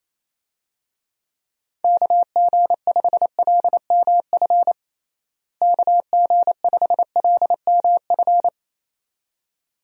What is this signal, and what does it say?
Morse code, heard as a keyed tone, about 28 wpm, 700 Hz. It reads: KG5LMF KG5LMF
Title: USA Call Signs - 6 Characters - Repeated 2x - 28wpm